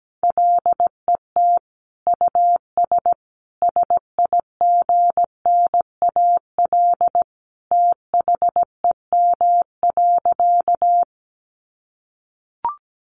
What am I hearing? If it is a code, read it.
Morse code, heard as a keyed tone, about 17 wpm, 700 Hz. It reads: LET US SIGNAL THEM.
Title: Sentences from Top 1400 Words 17wpm